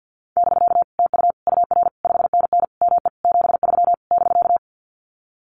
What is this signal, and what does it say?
Morse code, heard as a keyed tone, about 50 wpm, 700 Hz. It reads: <BK> TU UR 5NN ME 73 <BK>
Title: Common QSO Phrases Encore 50wpm